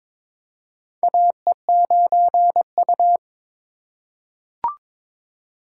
Morse code, heard as a keyed tone, about 21 wpm, 700 Hz. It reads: AE9U
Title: USA Call Signs - 4 Characters - 3x Character Spacing - 22wpm